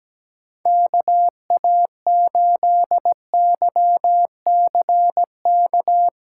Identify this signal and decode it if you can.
Morse code, heard as a keyed tone, about 17 wpm, 700 Hz. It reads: KA8YCK
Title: USA Callsign DE Callsign - 6 Characters - 17wpm